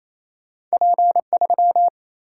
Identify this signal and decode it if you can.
Morse code, heard as a keyed tone, about 28 wpm, 700 Hz. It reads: P3
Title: Single-Letter Single-Number 28wpm